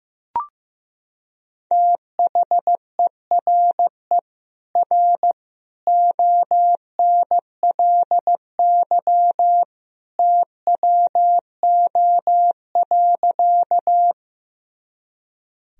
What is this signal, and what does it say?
Morse code, heard as a keyed tone, about 15 wpm, 700 Hz. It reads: THERE R ONLY TWO.
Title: Sentences from Top 200 Words and Abbreviations 15wpm